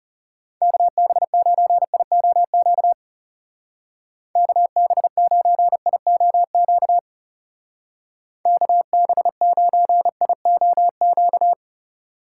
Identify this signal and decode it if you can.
Morse code, heard as a keyed tone, about 36 wpm, 700 Hz. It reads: KB9IOQ KB9IOQ KB9IOQ
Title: Speed Racing - USA 2x3 Call Signs 30wpm